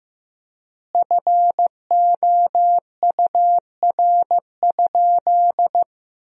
Morse code, heard as a keyed tone, about 15 wpm, 700 Hz. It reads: FOUR?
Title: 3x Word-Spacing - Sentences from Top 300 Words - 15wpm